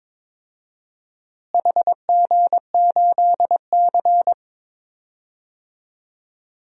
Morse code, heard as a keyed tone, about 21 wpm, 700 Hz. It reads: HG8C
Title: DXCC-100 Call Signs - 4 Characters - Repeated 3x - 22wpm